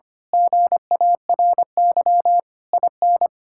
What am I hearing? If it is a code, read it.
Morse code, heard as a keyed tone, about 24 wpm, 700 Hz. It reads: GARY IN
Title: Speed-Racing Common QSO Phrases - 25wpm